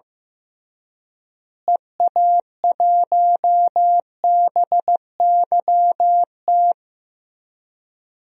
Morse code, heard as a keyed tone, about 15 wpm, 700 Hz. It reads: EA1BYT
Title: DXCC-100 Call Signs - 6 Characters - Repeated 3x - 15wpm